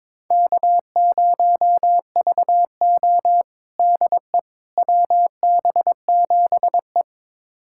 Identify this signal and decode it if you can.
Morse code, heard as a keyed tone, about 21 wpm, 700 Hz. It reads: K0VO DE WB7E
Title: USA Callsign DE Callsign - 4 Characters - 22wpm